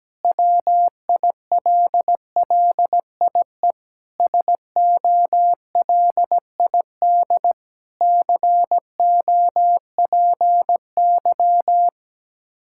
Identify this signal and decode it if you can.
Morse code, heard as a keyed tone, about 17 wpm, 700 Hz. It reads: WILLIE SOLID COPY
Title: Practice QSO - Exchange 3 - More Info - 17wpm